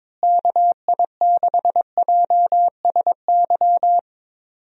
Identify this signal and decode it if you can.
Morse code, heard as a keyed tone, about 22 wpm, 700 Hz. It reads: KI6JSY
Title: USA Call Signs - 6 Characters - Repeated 2x - 22wpm